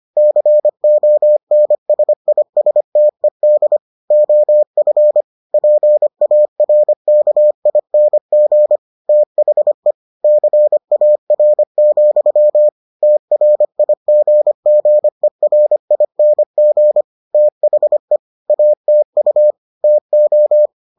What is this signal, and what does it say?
Morse code, heard as a keyed tone, about 25 wpm, 600 Hz. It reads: CONSISTED OF PARKING THE CAR, TRIGGERING THE ATU TO